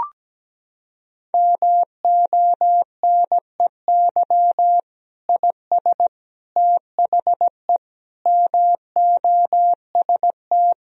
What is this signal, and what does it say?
Morse code, heard as a keyed tone, about 17 wpm, 700 Hz. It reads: MONEY IS THE MOST
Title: Sentences from Top 1000 Words - Review Part 1 - 17wpm